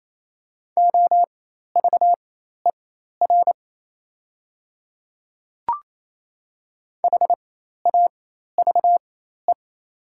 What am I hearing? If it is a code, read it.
Morse code, heard as a keyed tone, about 28 wpm, spaced as for 13 wpm, 700 Hz. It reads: OVER HAVE
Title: 4x Character Spacing - Top 100 Words - 28wpm